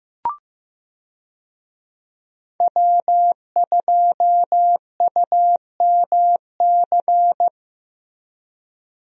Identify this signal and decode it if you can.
Morse code, heard as a keyed tone, about 15 wpm, 700 Hz. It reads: W2UMC
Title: USA Call Signs - 5 Characters - Repeated 3x - 15wpm